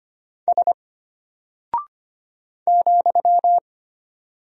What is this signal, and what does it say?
Morse code, heard as a keyed tone, about 25 wpm, 700 Hz. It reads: S ,
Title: Single Character (Letter:Number:Punctuation:Prosign) Alt 25wpm